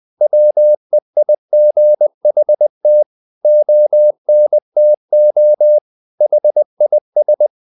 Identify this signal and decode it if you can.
Morse code, heard as a keyed tone, about 20 wpm, 600 Hz. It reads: WEIGHT ONTO HIS
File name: the old man and the sea part 2 20wpm